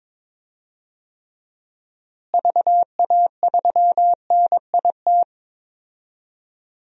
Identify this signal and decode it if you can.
Morse code, heard as a keyed tone, about 22 wpm, 700 Hz. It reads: VA3NIT
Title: DXCC-100 Call Signs - 6 Characters - Repeated 2x - 22wpm